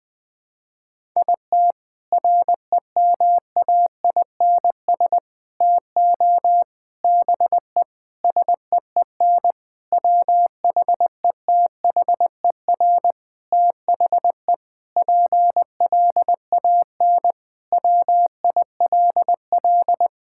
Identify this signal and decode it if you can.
Morse code, heard as a keyed tone, about 20 wpm, 700 Hz. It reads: IT REMAINS TO BE SEEN WHETHER THE PLAN WILL
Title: Sentences with 6-Word Phrases - 20wpm